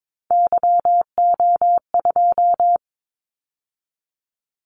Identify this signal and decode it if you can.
Morse code, heard as a keyed tone, about 22 wpm, 700 Hz. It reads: YO2
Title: Callsign Prefixes for Top 25 DXCC - 22wpm